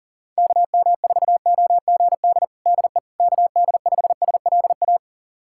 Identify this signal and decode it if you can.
Morse code, heard as a keyed tone, about 40 wpm, 700 Hz. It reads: KM4OGD DE KD5SLA
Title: USA Callsign DE Callsign - 6 Characters - 40wpm